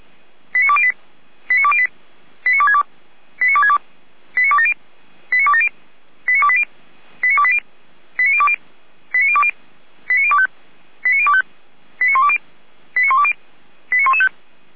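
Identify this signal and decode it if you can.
Five-tone selective call (ZVEI), heard as quick runs of five tones, tones of about 70 ms. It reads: 89208 89208 89262 89262 89270 89270 89280 89280 89020 89020 89025 89025 89120 89120 89115
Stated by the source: ZVEI five-tone call received by SDR.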